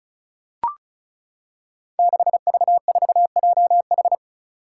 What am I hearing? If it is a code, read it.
Morse code, heard as a keyed tone, about 34 wpm, 700 Hz. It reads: 6V4JH
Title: Five Challenging Letter-Number Combinations 35wpm